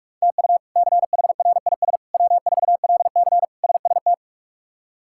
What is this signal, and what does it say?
Morse code, heard as a keyed tone, about 45 wpm, 700 Hz. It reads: TU CHRIS W4LC SST
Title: K1USN SST QSO Practice - 45wpm